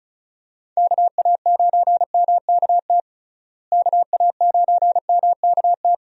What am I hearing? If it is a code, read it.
Morse code, heard as a keyed tone, about 36 wpm, 700 Hz. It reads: KA9MKT KA9MKT
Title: USA Call Signs - 6 Characters - Repeated 2x - 35wpm